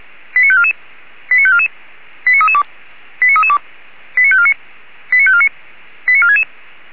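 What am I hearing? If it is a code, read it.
Five-tone selective call (ZVEI), heard as quick runs of five tones, tones of about 70 ms. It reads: 89544 89544 89332 89332 89549 89549 89466
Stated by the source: ZVEI five-tone selective call 89233 received by SDR, from the feuerwehr-orlamuende fire brigade audio server.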